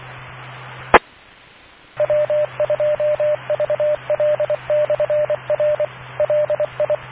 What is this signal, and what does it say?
Morse code, heard as a keyed tone, about 24 wpm, 600 Hz, with radio noise. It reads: E W2VL/R LI